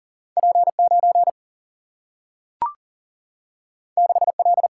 Morse code, heard as a keyed tone, about 40 wpm, 700 Hz. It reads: P 9 6 L
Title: Single-Letter Single-Number 40wpm